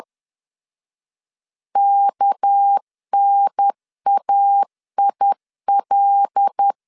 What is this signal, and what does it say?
Morse code, heard as a keyed tone, about 10 wpm, 800 Hz. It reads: KNAIL
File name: LEZIONI / 6 PKX0 / 6 PKX0 10wpm 2707